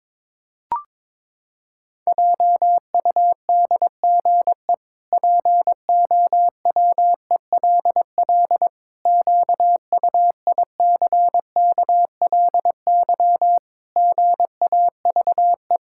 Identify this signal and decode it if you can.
Morse code, heard as a keyed tone, about 22 wpm, 700 Hz. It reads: JUDGE POWELL QUICKLY GAVE
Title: Pangram Challenge v2 - 22wpm